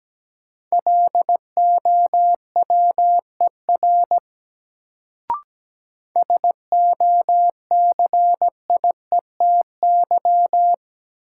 Morse code, heard as a keyed tone, about 17 wpm, 700 Hz. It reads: LOWER SOCIETY
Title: Sentences from Top 700 Words - New Word Drill - 17wpm